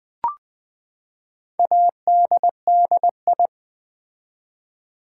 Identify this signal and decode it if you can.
Morse code, heard as a keyed tone, about 20 wpm, 700 Hz. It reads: ADDI
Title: Build A Word - 4 Letters At A Time - 12-Letter Words - 20wpm